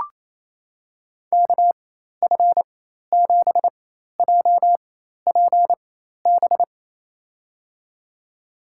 Morse code, heard as a keyed tone, about 28 wpm, spaced as for 13 wpm, 700 Hz. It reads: KF7JPB
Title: USA Call Signs - 6 Characters - 4x Character Spacing - 28wpm